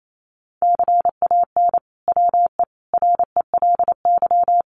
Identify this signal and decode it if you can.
Morse code, heard as a keyed tone, about 28 wpm, 700 Hz. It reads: CAN WE RELY
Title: Sentences from Top 1200 Words 28wpm